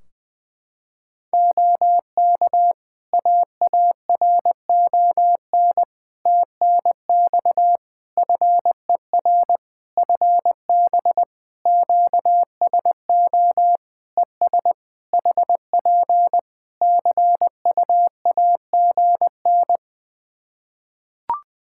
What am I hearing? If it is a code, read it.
Morse code, heard as a keyed tone, about 20 wpm, 700 Hz. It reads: OK AARON TNX FER FB QSO ES HP CUAGN
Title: K1USN SST QSO Practice 20wpm